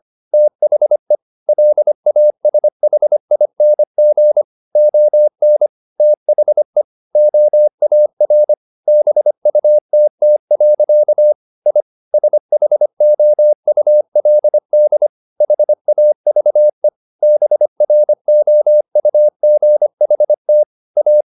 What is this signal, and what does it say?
Morse code, heard as a keyed tone, about 25 wpm, 600 Hz. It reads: THE LASHING ON THE OAR BUTT. I SHOULD HAVE BROUGHT A